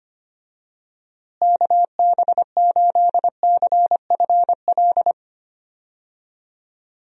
Morse code, heard as a keyed tone, about 25 wpm, 700 Hz. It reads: KB8CFL
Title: USA Call Signs - 6 Characters - Repeated 2x - 25wpm